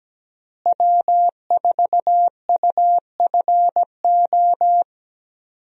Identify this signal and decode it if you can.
Morse code, heard as a keyed tone, about 17 wpm, 700 Hz. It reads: W4UFO
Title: USA Call Signs - 5 Characters - Repeated 2x - 17wpm